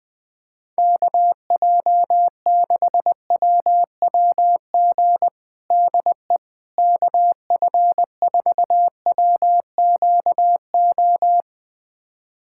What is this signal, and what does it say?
Morse code, heard as a keyed tone, about 20 wpm, 700 Hz. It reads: KJ6WWG DE KF4WQO
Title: USA Callsign DE Callsign - 6 Characters - 20wpm